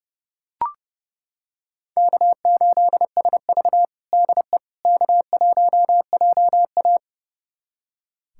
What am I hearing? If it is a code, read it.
Morse code, heard as a keyed tone, about 32 wpm, 700 Hz. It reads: K8SV DE K1JA
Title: USA Callsign DE Callsign - 4 Characters - 30wpm